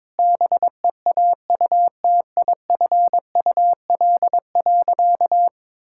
Morse code, heard as a keyed tone, about 22 wpm, 700 Hz. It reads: BEAUTIFUL.
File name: practice / 22wpm / Sentences with 2-Word Phrases - 22wpm